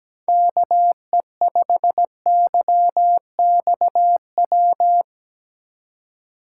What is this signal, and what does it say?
Morse code, heard as a keyed tone, about 19 wpm, 700 Hz. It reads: KE5YXW
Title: Speed Racing - USA 2x3 Call Signs 17wpm